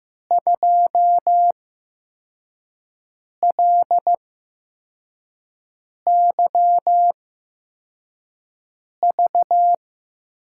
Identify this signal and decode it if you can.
Morse code, heard as a keyed tone, about 15 wpm, spaced as for 4 wpm, 700 Hz. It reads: 2LYV
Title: USA Call Signs - 8x Character Spacing - 15wpm